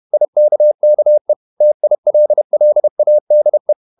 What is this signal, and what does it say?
Morse code, heard as a keyed tone, about 31 wpm, 600 Hz. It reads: IKKE TILLADE
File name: Ferie QRQ 31wpm